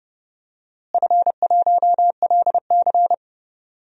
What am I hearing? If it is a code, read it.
Morse code, heard as a keyed tone, about 29 wpm, 700 Hz. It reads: F1LC
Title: DXCC-100 Call Signs - 4 Characters - Repeated 3x - 30wpm